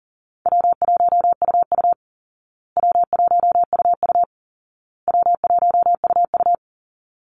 Morse code, heard as a keyed tone, about 40 wpm, 700 Hz. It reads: W1UU W1UU W1UU
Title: Call Signs Repeated 2x - 40wpm